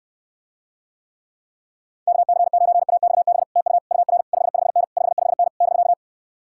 Morse code, heard as a keyed tone, about 67 wpm, 700 Hz. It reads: KC9AXB TU UR 55N 55N <BK>